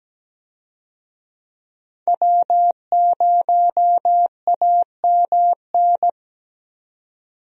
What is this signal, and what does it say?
Morse code, heard as a keyed tone, about 17 wpm, 700 Hz. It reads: W0AMN
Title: USA Call Signs - 5 Characters - 17wpm